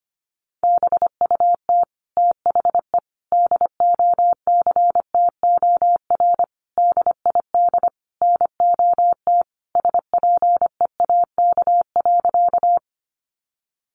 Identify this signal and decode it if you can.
Morse code, heard as a keyed tone, about 25 wpm, 700 Hz. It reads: BUT THE DOCTOR DID NOT SPEAK.
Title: Sentences from Top 1500 Words 25wpm